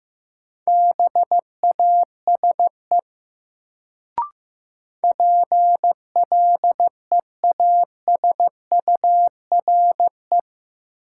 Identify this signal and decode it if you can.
Morse code, heard as a keyed tone, about 15 wpm, 700 Hz. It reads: BASE PLEASURE
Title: Sentences from Top 1600 Words - New Word Drill Alt - 15wpm